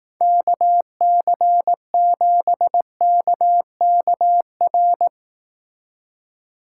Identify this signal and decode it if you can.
Morse code, heard as a keyed tone, about 18 wpm, 700 Hz. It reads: KC7KKR
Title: Speed Racing - USA 2x3 Call Signs 15wpm